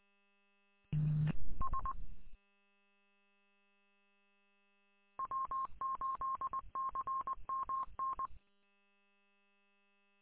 Morse code, heard as a keyed tone, about 20 wpm, 1070 Hz. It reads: S W8CMN
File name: w8cmn- -combined